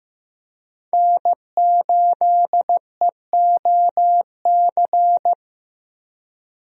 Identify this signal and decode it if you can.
Morse code, heard as a keyed tone, about 15 wpm, 700 Hz. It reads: N8EOC
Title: USA Call Signs - 5 Characters - Repeated 2x - 15wpm